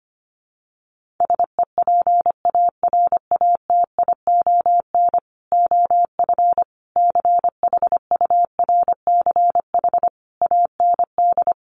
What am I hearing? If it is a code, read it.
Morse code, heard as a keyed tone, about 25 wpm, 700 Hz. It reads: SEPARATION OF CHURCH AND
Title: Sets of 5 Words 25wpm